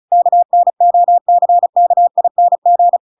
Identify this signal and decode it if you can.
Morse code, heard as a keyed tone, about 34 wpm, 700 Hz. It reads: KNOCKING